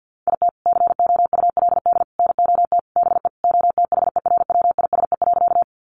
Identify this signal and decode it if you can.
Morse code, heard as a keyed tone, about 50 wpm, 700 Hz. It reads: IT COULD NOT BE OTHERWISE.
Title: Sentences from Top 1200 Words 50wpm